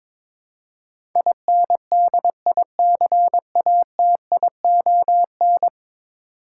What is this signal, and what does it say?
Morse code, heard as a keyed tone, about 22 wpm, 700 Hz. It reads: INDICATION
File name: practice / 22wpm / 10 Letter Words - First 3 Letters Given - 22wpm